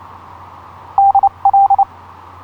Morse code, heard as a keyed tone, about 29 wpm, spaced as for 25 wpm, 790 Hz. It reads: DL
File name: zl1byz ha8wx etc 12m cw 06nov2010 0858